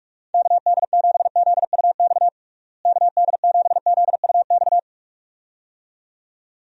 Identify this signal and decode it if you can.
Morse code, heard as a keyed tone, about 41 wpm, 700 Hz. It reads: KD7ZUX KD7ZUX
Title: USA Call Signs - 6 Characters - Repeated 2x - 45wpm